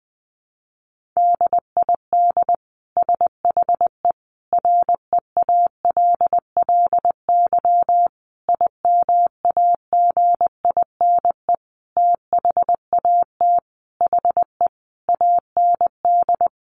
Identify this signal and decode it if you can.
Morse code, heard as a keyed tone, about 19 wpm, 700 Hz. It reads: DID SHE REALLY IMAGINE THAT HE AND
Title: Sentences from Top 1400 Words 20wpm